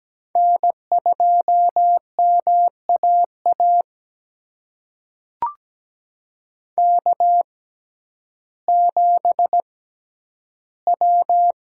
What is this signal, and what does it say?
Morse code, heard as a keyed tone, about 17 wpm, spaced as for 10 wpm, 700 Hz. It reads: N2MAA K7W
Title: USA Call Signs - 5 Characters - 6x Character Spacing - 17wpm